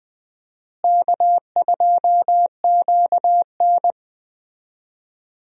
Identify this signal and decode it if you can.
Morse code, heard as a keyed tone, about 20 wpm, 700 Hz. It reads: K2QN
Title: USA Call Signs - 4 Characters - Repeated 2x - 20wpm